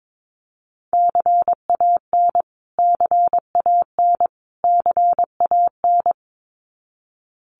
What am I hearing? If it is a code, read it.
Morse code, heard as a keyed tone, about 22 wpm, 700 Hz. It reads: CAN CAN CAN
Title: Top 500 Words Repeated 22wpm